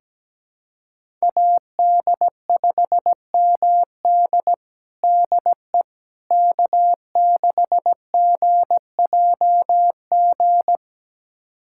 Morse code, heard as a keyed tone, about 17 wpm, 700 Hz. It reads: AD5MD DE K6GJG
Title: USA Callsign DE Callsign - 5 Characters - 17wpm